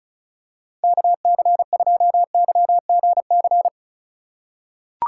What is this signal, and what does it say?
Morse code, heard as a keyed tone, about 36 wpm, 700 Hz. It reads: KC2YGC
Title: USA Call Signs - 4x Character Spacing - 35wpm